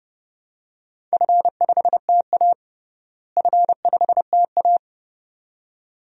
Morse code, heard as a keyed tone, about 32 wpm, 700 Hz. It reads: F5TA F5TA
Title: DXCC-100 Call Signs - 4 Characters - Repeated 2x - 30wpm